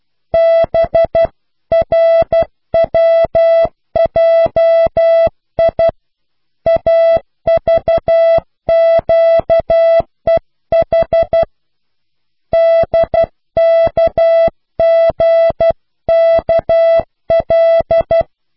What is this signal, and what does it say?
Morse code, heard as a keyed tone, about 12 wpm, 660 Hz. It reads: BRWJI AVQEH DKGKL